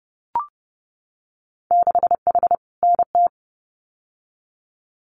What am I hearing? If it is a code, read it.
Morse code, heard as a keyed tone, about 30 wpm, 700 Hz. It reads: 6H NT
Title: Winter Field Day Exchanges Encore 30wpm